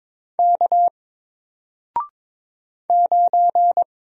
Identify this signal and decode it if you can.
Morse code, heard as a keyed tone, about 22 wpm, 700 Hz. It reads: K 9
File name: Single-Letter-Number-22wpm